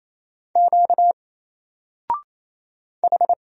Morse code, heard as a keyed tone, about 28 wpm, 700 Hz. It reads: Q H
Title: Single Letter-Number Alt 28wpm